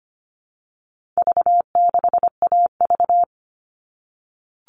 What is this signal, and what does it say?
Morse code, heard as a keyed tone, about 25 wpm, 700 Hz. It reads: V6AV
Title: All Cut Numbers - 4-Digit 25wpm